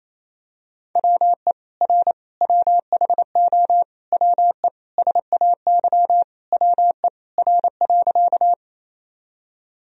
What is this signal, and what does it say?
Morse code, heard as a keyed tone, about 28 wpm, 700 Hz. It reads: WE R WHO WE SAY WE R.
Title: Sentences from Top 200 Words and Abbreviations 28wpm